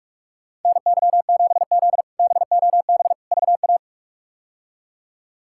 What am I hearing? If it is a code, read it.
Morse code, heard as a keyed tone, about 47 wpm, 700 Hz. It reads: NY7Z BOB VA
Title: K1USN SST QSO Practice - 45wpm